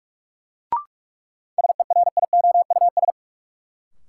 Morse code, heard as a keyed tone, about 46 wpm, 700 Hz. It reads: SERIOUS
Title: Sentences from Top 1100 Words - New Word Drill - 45wpm